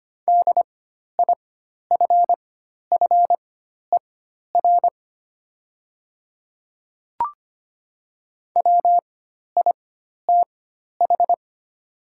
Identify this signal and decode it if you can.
Morse code, heard as a keyed tone, about 25 wpm, spaced as for 12 wpm, 690 Hz. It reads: DIFFER WITH
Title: 4x Character Spacing - Top 200 Words - 25wpm